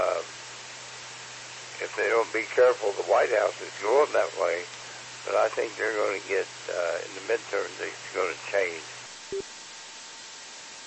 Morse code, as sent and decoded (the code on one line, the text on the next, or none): none